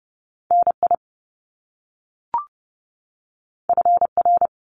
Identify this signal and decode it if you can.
Morse code, heard as a keyed tone, about 30 wpm, 700 Hz. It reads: NI FR
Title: Top Two Letter Combinations in Words 30wpm